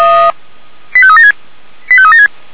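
Five-tone selective call (ZVEI), heard as quick runs of five tones, tones of about 70 ms. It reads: 85376
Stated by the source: ZVEI five-tone call 89273 received on SDR.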